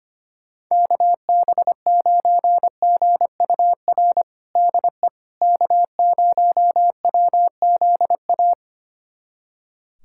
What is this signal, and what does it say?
Morse code, heard as a keyed tone, about 25 wpm, 700 Hz. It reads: KB9GUR DE K0WZA
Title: USA Callsign DE Callsign 25wpm